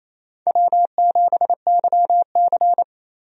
Morse code, heard as a keyed tone, about 27 wpm, 700 Hz. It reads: W7YC
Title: USA Call Signs - 4 Characters - 28wpm